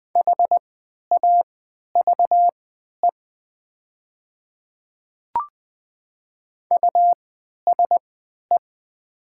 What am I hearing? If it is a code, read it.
Morse code, heard as a keyed tone, about 20 wpm, spaced as for 11 wpm, 700 Hz. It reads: HAVE USE
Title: 3x Character Spacing - Top 100 Words - 20wpm